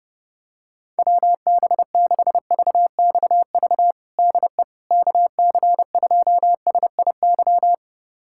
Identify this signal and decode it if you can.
Morse code, heard as a keyed tone, about 32 wpm, 700 Hz. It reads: WB6VXV DE KC2SIY
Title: USA Callsign DE Callsign 30wpm